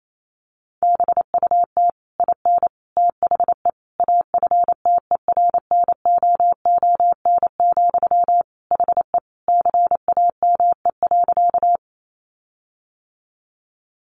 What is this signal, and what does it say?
Morse code, heard as a keyed tone, about 28 wpm, 700 Hz. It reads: BUT IN THE AFTERNOON, HE CAME.
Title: Sentences from Top 1500 Words 28wpm